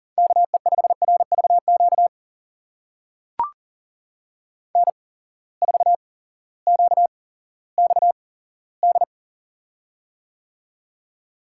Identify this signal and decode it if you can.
Morse code, heard as a keyed tone, about 40 wpm, spaced as for 18 wpm, 700 Hz. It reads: KE5RVQ N4QXD
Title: USA Call Signs - 8x Character Spacing - 40wpm